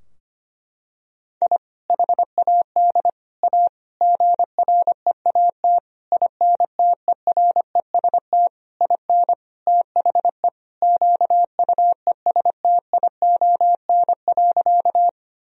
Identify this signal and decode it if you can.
Morse code, heard as a keyed tone, about 25 wpm, 700 Hz. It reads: I HAD A GREAT INTEREST IN THE QUESTION.
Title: Sentences from Top 1000 Words - Review Part 2 - 25wpm